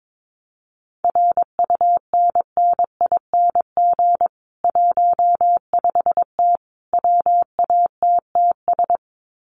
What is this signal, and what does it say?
Morse code, heard as a keyed tone, about 22 wpm, 700 Hz. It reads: RUNNING 15T WATTS
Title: Common QSO Phrases 22wpm